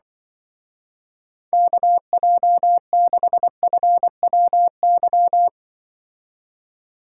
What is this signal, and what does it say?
Morse code, heard as a keyed tone, about 25 wpm, 700 Hz. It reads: KJ6FWY
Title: Speed Racing - USA Call Signs 20wpm